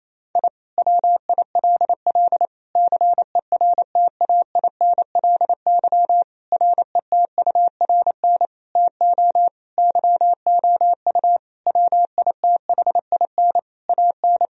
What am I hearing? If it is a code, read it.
Morse code, heard as a keyed tone, about 28 wpm, 700 Hz. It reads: I WILL CERTAINLY RETURN TO YOU WITHIN AN
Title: Sentences from Top 1000 Words - Review Part 4 - 28wpm